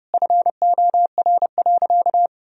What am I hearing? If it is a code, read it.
Morse code, heard as a keyed tone, about 30 wpm, 700 Hz. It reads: FOR.
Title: Sentences from Top 1000 Words - Review Part 3 - 30wpm